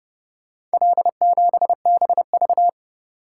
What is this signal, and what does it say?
Morse code, heard as a keyed tone, about 29 wpm, 700 Hz. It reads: L7BV